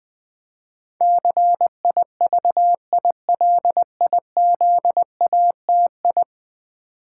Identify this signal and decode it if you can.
Morse code, heard as a keyed tone, about 20 wpm, 700 Hz. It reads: CIVILIZATI
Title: Build A Word - 5 Letters At A Time - 12-Letter Words - 20wpm